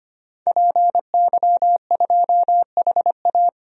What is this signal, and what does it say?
Morse code, heard as a keyed tone, about 25 wpm, 700 Hz. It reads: PY2HA
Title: DXCC-100 Call Signs - 5 Characters - 25wpm